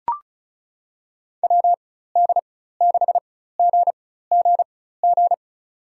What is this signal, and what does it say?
Morse code, heard as a keyed tone, about 36 wpm, spaced as for 16 wpm, 700 Hz. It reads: WD6GGG
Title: USA Call Signs - 4x Character Spacing - 35wpm